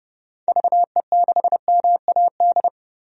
Morse code, heard as a keyed tone, about 30 wpm, 700 Hz. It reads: VE6MAD
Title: DXCC-100 Call Signs - 6 Characters - 30wpm